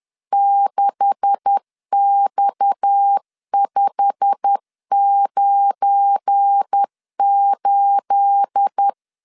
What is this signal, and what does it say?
Morse code, heard as a keyed tone, about 10 wpm, 800 Hz. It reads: 6X598
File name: LEZIONI / 9 286 10wpm 2735